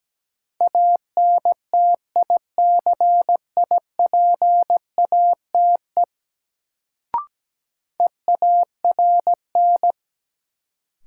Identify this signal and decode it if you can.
Morse code, heard as a keyed tone, about 17 wpm, 700 Hz. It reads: ANTICIPATE EARN
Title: Sentences from Top 1500 Words - New Word Drill - 17wpm